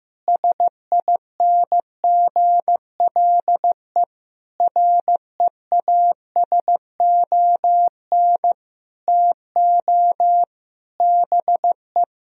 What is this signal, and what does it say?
Morse code, heard as a keyed tone, about 15 wpm, 700 Hz. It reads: SINGLE REASON TO BE
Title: Sentences from Top 1000 Words - Review Part 4 - 15wpm